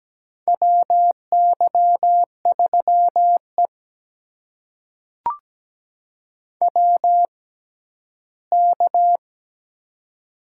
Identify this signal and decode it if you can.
Morse code, heard as a keyed tone, about 17 wpm, spaced as for 12 wpm, 700 Hz. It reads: WY3E WK
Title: USA Call Signs - 4 Characters - 6x Character Spacing - 17wpm